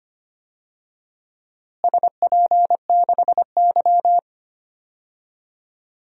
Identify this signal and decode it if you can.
Morse code, heard as a keyed tone, about 25 wpm, 700 Hz. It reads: SP6Y
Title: DXCC-100 Call Signs - 4 Characters - Repeated 2x - 25wpm